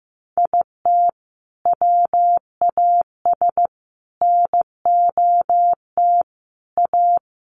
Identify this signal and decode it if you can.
Morse code, heard as a keyed tone, about 15 wpm, 700 Hz. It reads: IT WAS NOT A
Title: Sentences from Top 1000 Words 15wpm